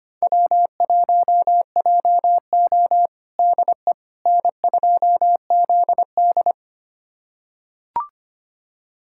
Morse code, heard as a keyed tone, about 25 wpm, 700 Hz. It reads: W1JO DE N2ZD
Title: USA Callsign DE Callsign - 4 Characters - 25wpm